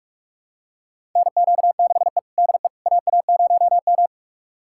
Morse code, heard as a keyed tone, about 46 wpm, 700 Hz. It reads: NQ6E DE AA0M
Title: USA Callsign DE Callsign - 4 Characters - 45wpm